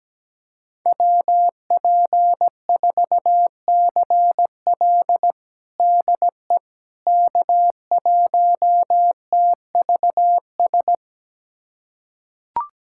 Morse code, heard as a keyed tone, about 17 wpm, 700 Hz. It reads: WP4CL DE K1TVS
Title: USA Callsign DE Callsign - 5 Characters - 17wpm